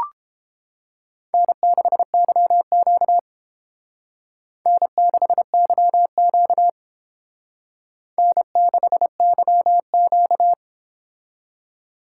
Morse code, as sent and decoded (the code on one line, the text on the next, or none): -. -.... -.-- --.- / -. -.... -.-- --.- / -. -.... -.-- --.-
N6YQ N6YQ N6YQ